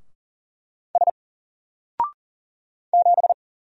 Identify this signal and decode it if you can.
Morse code, heard as a keyed tone, about 40 wpm, 700 Hz. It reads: S 7
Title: Single Letter-Number 40wpm